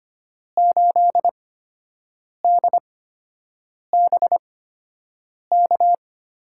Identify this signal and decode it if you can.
Morse code, heard as a keyed tone, about 25 wpm, spaced as for 7 wpm, 700 Hz. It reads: 8DBK
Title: USA Call Signs - 5 Characters - 8x Character Spacing - 25wpm